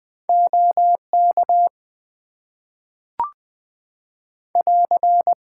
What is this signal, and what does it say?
Morse code, heard as a keyed tone, about 20 wpm, 700 Hz. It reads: OK <AR>
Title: Top 90 QSO Elements - 6x Character Spacing - 20wpm